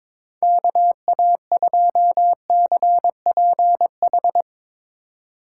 Morse code, heard as a keyed tone, about 22 wpm, 700 Hz. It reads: KA2CPH
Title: Speed Racing - USA 2x3 Call Signs 22wpm